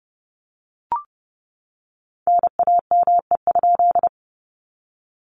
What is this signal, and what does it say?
Morse code, heard as a keyed tone, about 30 wpm, 700 Hz. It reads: NAME?
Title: Common QSO Phrases Encore 30wpm